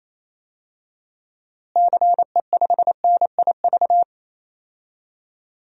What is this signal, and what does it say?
Morse code, heard as a keyed tone, about 28 wpm, 700 Hz. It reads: CE5NIV
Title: DXCC-100 Call Signs - 6 Characters - Repeated 3x - 28wpm